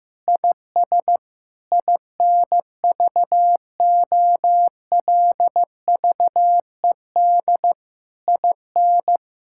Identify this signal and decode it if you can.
Morse code, heard as a keyed tone, about 15 wpm, 700 Hz. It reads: IS INVOLVED IN
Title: Sentences from Top 1000 Words - Review Part 5 - 15wpm